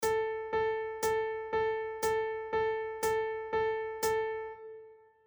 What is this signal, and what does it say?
There is no code.